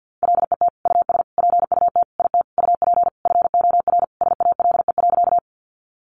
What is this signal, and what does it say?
Morse code, heard as a keyed tone, about 50 wpm, 700 Hz. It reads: LET US PUT IT UP FOR SALE.
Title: Sentences from Top 1400 Words 50wpm